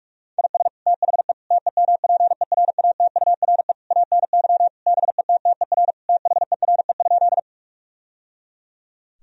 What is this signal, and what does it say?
Morse code, heard as a keyed tone, about 46 wpm, 700 Hz. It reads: IS THE TEMPERATURE ANY BETTER THERE?
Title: Sentences from Top 1000 Words - Review Part 4 - 45wpm